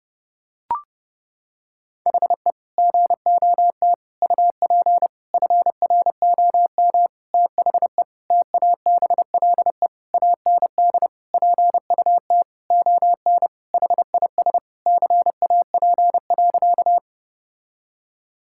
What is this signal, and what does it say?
Morse code, heard as a keyed tone, about 30 wpm, 700 Hz. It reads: HE GOT UP FROM THE TABLE AND PUT ON HIS CAP.
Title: Sentences from Top 1600 Words 30wpm